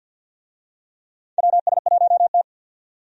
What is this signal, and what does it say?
Morse code, heard as a keyed tone, about 50 wpm, 700 Hz. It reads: WS1T
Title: USA Call Signs - 4 Characters - 50wpm